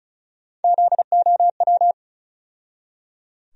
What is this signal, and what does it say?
Morse code, heard as a keyed tone, about 36 wpm, 700 Hz. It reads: ZOW
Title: USA Call Sign Postfixes - 3 Letters 35wpm